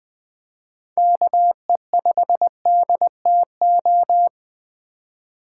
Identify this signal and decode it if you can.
Morse code, heard as a keyed tone, about 24 wpm, 700 Hz. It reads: KE5DTO
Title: Speed Racing - USA 2x3 Call Signs 20wpm